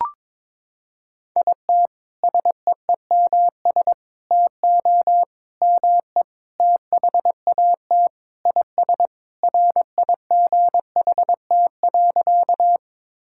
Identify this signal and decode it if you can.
Morse code, heard as a keyed tone, about 22 wpm, 700 Hz. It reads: IT SEEMS TO ME THAT IS RIGHT.
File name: practice / 22wpm / Sentences with 4-Word Phrases - 22wpm